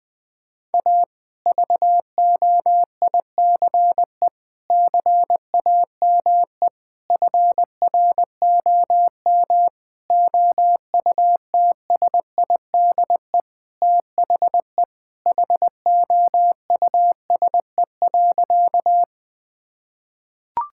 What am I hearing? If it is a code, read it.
Morse code, heard as a keyed tone, about 20 wpm, 700 Hz. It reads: A VOICE CAME FROM OUTSIDE THE HOUSE.
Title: Sentences from Top 1000 Words - Review Part 6 - 20wpm